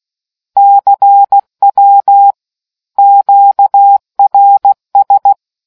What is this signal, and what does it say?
Morse code, heard as a keyed tone, about 16 wpm, 800 Hz. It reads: CW QRS